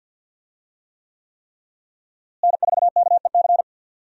Morse code, heard as a keyed tone, about 53 wpm, 700 Hz. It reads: N4KEC
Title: USA Call Signs - 5 Characters - Repeated 3x - 50wpm